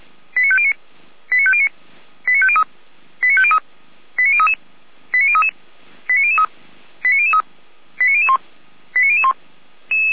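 Five-tone selective call (ZVEI), heard as quick runs of five tones, tones of about 70 ms. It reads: 89509 89509 89503 89503 89033 89033 89003 89003 89001 89001
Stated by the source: Five-tone ZVEI selective call received on an SDR.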